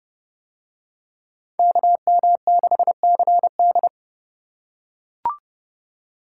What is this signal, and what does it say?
Morse code, heard as a keyed tone, about 29 wpm, 700 Hz. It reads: KM6CD